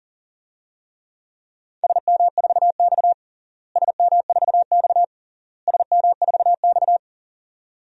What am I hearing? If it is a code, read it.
Morse code, heard as a keyed tone, about 37 wpm, 700 Hz. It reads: SM4X SM4X SM4X
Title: DXCC-100 Call Signs - 4 Characters - Repeated 3x - 40wpm